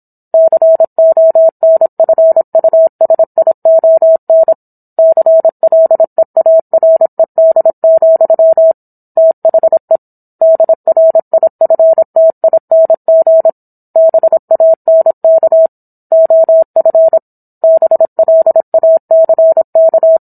Morse code, heard as a keyed tone, about 27 wpm, 650 Hz. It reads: CONFUSION CLEARED, THE DRIFTING BANK OF BLACK